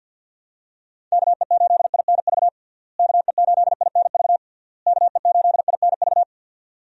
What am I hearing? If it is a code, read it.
Morse code, heard as a keyed tone, about 49 wpm, 700 Hz. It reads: KE8INV KE8INV KE8INV
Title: USA Call Signs - 6 Characters - Repeated 3x - 50wpm